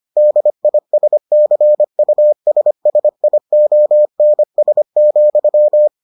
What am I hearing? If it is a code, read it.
Morse code, heard as a keyed tone, about 25 wpm, 600 Hz. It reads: DISCUSSIONS,